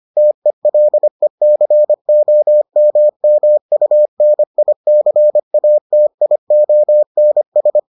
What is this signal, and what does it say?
Morse code, heard as a keyed tone, about 25 wpm, 600 Hz. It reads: TELECOMMUNICATIONS